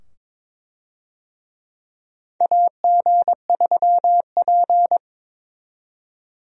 Morse code, heard as a keyed tone, about 22 wpm, 700 Hz. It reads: AG3P
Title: USA Call Signs - 4 Characters - 22wpm